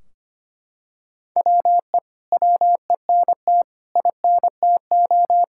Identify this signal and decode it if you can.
Morse code, heard as a keyed tone, about 25 wpm, 700 Hz. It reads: WE WENT INTO
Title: Sentences from Top 1600 Words 25wpm